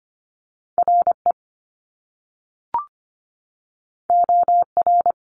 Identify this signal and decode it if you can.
Morse code, heard as a keyed tone, about 25 wpm, 700 Hz. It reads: RE OR
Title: Top Two Letter Combinations in Words 25wpm